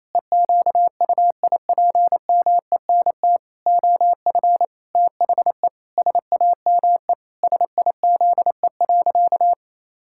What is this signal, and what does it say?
Morse code, heard as a keyed tone, about 28 wpm, 700 Hz. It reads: EQUIPMENT OF THE SAME SIZE.
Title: Pangram Challenge v2 - 28wpm